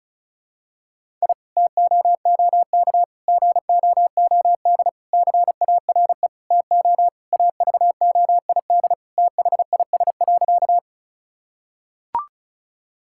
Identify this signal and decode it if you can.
Morse code, heard as a keyed tone, about 36 wpm, 700 Hz. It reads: I TOOK GOOD CARE TO AVOID THIS.
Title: Sentences from Top 1000 Words - Review Part 6 - 35wpm